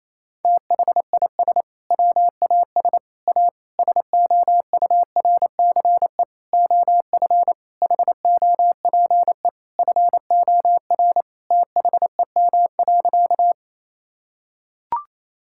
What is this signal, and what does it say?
Morse code, heard as a keyed tone, about 28 wpm, 700 Hz. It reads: THIS WAS A SOURCE OF HOPE FOR THEM.
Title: Sentences from Top 1000 Words - Review Part 5 - 28wpm